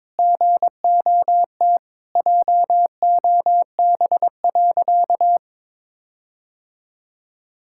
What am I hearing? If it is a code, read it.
Morse code, heard as a keyed tone, about 22 wpm, 700 Hz. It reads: GOT JOB.
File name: practice / 22wpm / Sentences with 4-Word Phrases - 22wpm